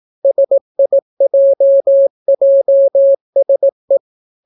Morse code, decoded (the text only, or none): SIJJSE